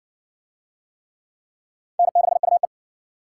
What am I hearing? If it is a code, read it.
Morse code, heard as a keyed tone, about 59 wpm, 700 Hz. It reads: N6UE